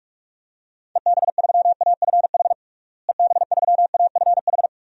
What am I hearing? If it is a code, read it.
Morse code, heard as a keyed tone, about 45 wpm, 700 Hz. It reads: EB3AFH EB3AFH
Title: DXCC-100 Call Signs - 6 Characters - Repeated 2x - 45wpm